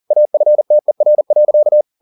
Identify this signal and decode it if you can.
Morse code, heard as a keyed tone, about 40 wpm, 600 Hz. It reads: AFTER.
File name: the old man and the sea 2 40wpm